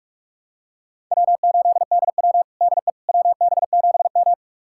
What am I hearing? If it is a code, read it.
Morse code, heard as a keyed tone, about 47 wpm, 700 Hz. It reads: W8DW DE WB7M
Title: USA Callsign DE Callsign - 4 Characters - 45wpm